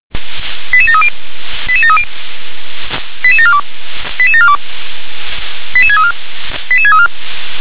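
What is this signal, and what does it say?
Five-tone selective call (ZVEI), heard as quick runs of five tones, tones of about 80 ms. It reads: 88730 88730 88742 88742 88634 88634
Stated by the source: Received on an SDR, ZVEI five-tone selective call, channel 357.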